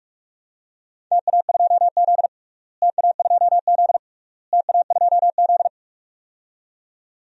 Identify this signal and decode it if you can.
Morse code, heard as a keyed tone, about 42 wpm, 700 Hz. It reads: TA2Z TA2Z TA2Z
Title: DXCC-100 Call Signs - 4 Characters - Repeated 3x - 45wpm